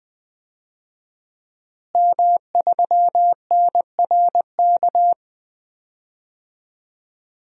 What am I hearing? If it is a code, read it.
Morse code, heard as a keyed tone, about 20 wpm, 700 Hz. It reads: M3NRK
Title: DXCC-100 Call Signs - 5 Characters - Repeated 3x - 20wpm